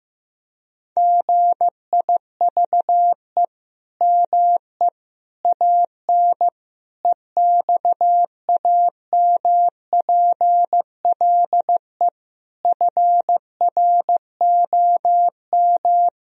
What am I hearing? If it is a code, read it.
Morse code, heard as a keyed tone, about 15 wpm, 700 Hz. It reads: GIVE ME AN EXAMPLE FROM
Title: Sentences from Top 1000 Words - Review Part 3 - 15wpm